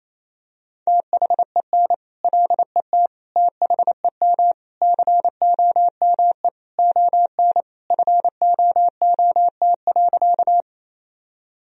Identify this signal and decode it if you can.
Morse code, heard as a keyed tone, about 27 wpm, 700 Hz. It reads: THEN LET THEM COME ON FOOT.
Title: Sentences from Top 1000 Words - Review Part 3 - 28wpm